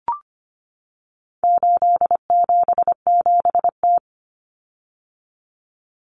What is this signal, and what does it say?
Morse code, heard as a keyed tone, about 25 wpm, 700 Hz. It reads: 877T
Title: Common Cut Numbers - 4-Digit 25wpm